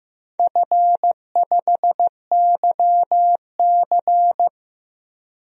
Morse code, heard as a keyed tone, about 15 wpm, 700 Hz. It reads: F5YC
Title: DXCC-100 Call Signs - 4 Characters - 15wpm